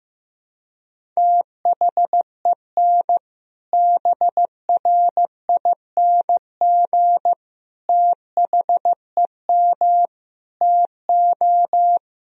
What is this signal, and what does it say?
Morse code, heard as a keyed tone, about 15 wpm, 700 Hz. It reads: THEN BRING THEM TO
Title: Sentences from Top 500 Words and Abbreviations 15wpm